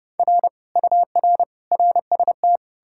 Morse code, heard as a keyed tone, about 40 wpm, spaced as for 35 wpm, 700 Hz. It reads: R UR RST
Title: Speed-Racing Common QSO Phrases Encore - 30wpm